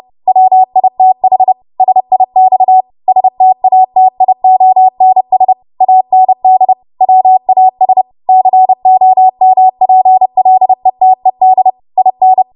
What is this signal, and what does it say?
Morse code, heard as a keyed tone, about 30 wpm, 750 Hz. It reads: WITH SIX STATIONS AND WAS COMPLETED IN
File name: Archive / 30 WPM / 30WPM